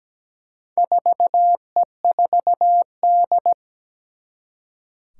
Morse code, heard as a keyed tone, about 17 wpm, 700 Hz. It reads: 4E4D
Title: Four Challenging Letter-Number Combinations 17wpm